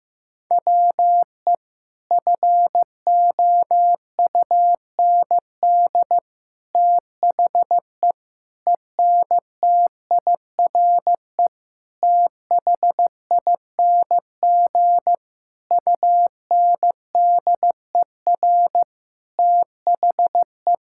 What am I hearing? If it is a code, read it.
Morse code, heard as a keyed tone, about 15 wpm, 700 Hz. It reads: WE FOUND THE ENTIRE THING UNDER THE
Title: Sentences from Top 1000 Words - Review Part 5 - 15wpm